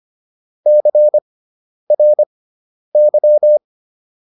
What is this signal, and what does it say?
Morse code, heard as a keyed tone, about 25 wpm, spaced as for 10 wpm, 600 Hz. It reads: CRY